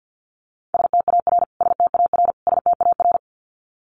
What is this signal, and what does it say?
Morse code, heard as a keyed tone, about 50 wpm, 700 Hz. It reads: STAR STAR STAR
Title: Top 500 Words Repeated 50wpm